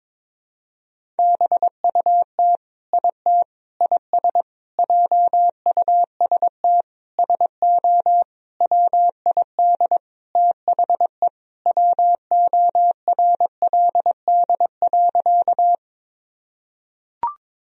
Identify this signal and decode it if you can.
Morse code, heard as a keyed tone, about 22 wpm, 700 Hz. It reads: BUT IT IS JUST SO WID THE WORLD.
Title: Sentences from Top 200 Words and Abbreviations 22wpm